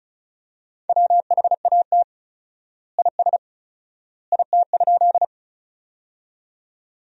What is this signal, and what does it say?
Morse code, heard as a keyed tone, about 36 wpm, 700 Hz. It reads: WHAT IS IT?
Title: 4x Word-Spacing - Sentences from Top 100 Words - 35wpm